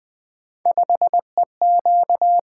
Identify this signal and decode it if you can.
Morse code, heard as a keyed tone, about 20 wpm, 700 Hz. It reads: 5EQ
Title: Three Letter-Number Combinations - 20wpm